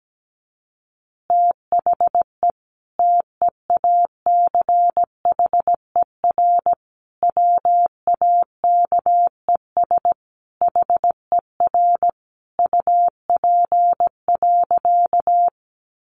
Morse code, heard as a keyed tone, about 17 wpm, 700 Hz. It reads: THE TEACHER WAKES HER UP.
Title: Easy Phrases 17wpm